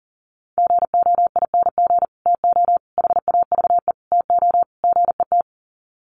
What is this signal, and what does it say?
Morse code, heard as a keyed tone, about 40 wpm, 700 Hz. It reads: GOING TO HAVE TO GET
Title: Sets of 5 Words Encore2 40wpm